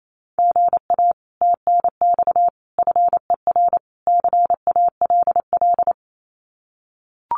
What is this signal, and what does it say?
Morse code, heard as a keyed tone, about 28 wpm, 700 Hz. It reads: GA TNX FER CALL
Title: Common QSO Phrases Encore 28wpm